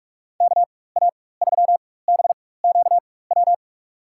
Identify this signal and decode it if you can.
Morse code, heard as a keyed tone, about 46 wpm, spaced as for 21 wpm, 700 Hz. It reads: KA3BQW
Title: USA Call Signs - 6 Characters - 4x Character Spacing - 45wpm